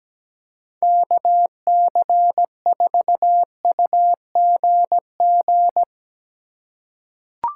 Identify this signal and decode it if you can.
Morse code, heard as a keyed tone, about 17 wpm, 700 Hz. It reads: KC4UGG
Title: USA Call Signs - 6 Characters - 4x Character Spacing - 17wpm